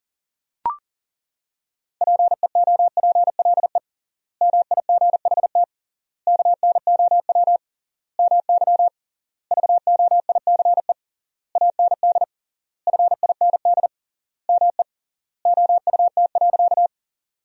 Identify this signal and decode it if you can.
Morse code, heard as a keyed tone, about 40 wpm, 700 Hz. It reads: PEOPLE MIGHT KNOW MY VOICE AND FIND ME OUT.
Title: 3x Word-Spacing - Sentences from Top 500 Words - 40wpm